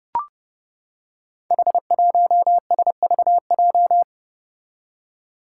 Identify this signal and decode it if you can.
Morse code, heard as a keyed tone, about 32 wpm, 700 Hz. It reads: H1SVJ
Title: Five Challenging Letter-Number Combinations 30wpm